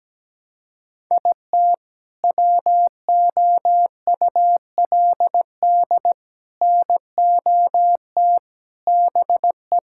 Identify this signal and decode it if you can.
Morse code, heard as a keyed tone, about 17 wpm, 700 Hz. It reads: IT WOULD NOT BE
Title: Sentences from Top 1600 Words 17wpm